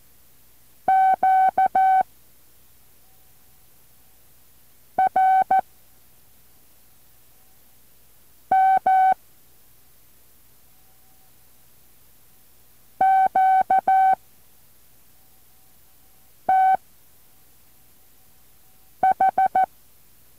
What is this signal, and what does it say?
Morse code, heard as a keyed tone, about 14 wpm, 760 Hz. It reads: Q R M Q T H
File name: intl morse code2